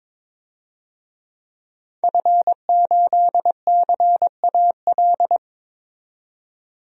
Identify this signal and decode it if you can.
Morse code, heard as a keyed tone, about 22 wpm, 700 Hz. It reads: F8CAL
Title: DXCC-100 Call Signs - 5 Characters - 22wpm